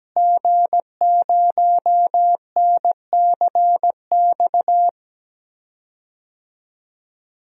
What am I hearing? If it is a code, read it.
Morse code, heard as a keyed tone, about 17 wpm, 700 Hz. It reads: G0NCX
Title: DXCC-100 Call Signs - 5 Characters - Repeated 3x - 17wpm